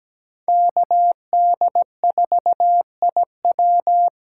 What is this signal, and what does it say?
Morse code, heard as a keyed tone, about 17 wpm, 700 Hz. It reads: KD4IW
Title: USA Call Signs - 5 Characters - 4x Character Spacing - 17wpm